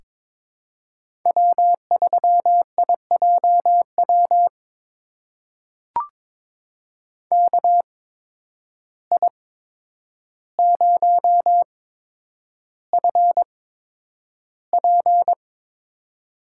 Morse code, heard as a keyed tone, about 22 wpm, spaced as for 9 wpm, 700 Hz. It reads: W3IJW KI0FP
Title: USA Call Signs - 5 Characters - 8x Character Spacing - 22wpm